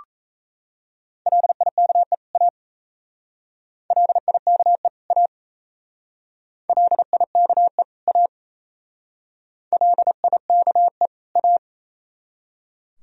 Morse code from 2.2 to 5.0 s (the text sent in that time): A LIKE